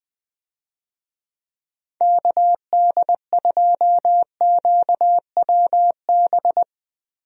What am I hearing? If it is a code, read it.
Morse code, heard as a keyed tone, about 21 wpm, 700 Hz. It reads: KD2QWB
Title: USA Call Signs - 6 Characters - Repeated 3x - 20wpm